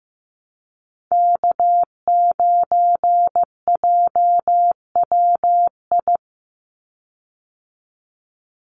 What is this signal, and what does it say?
Morse code, heard as a keyed tone, about 15 wpm, 700 Hz. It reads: K9JWI
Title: Call Signs Repeated 2x - 15wpm